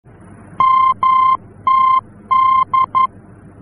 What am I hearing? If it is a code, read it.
Morse code, heard as a keyed tone, about 11 wpm, 1050 Hz. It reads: MTD
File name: france / mx / MTD-F-VOR